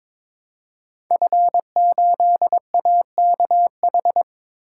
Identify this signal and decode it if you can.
Morse code, heard as a keyed tone, about 23 wpm, 700 Hz. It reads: F8AKH
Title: Code Groups - 5 Letter:Numbers - 22wpm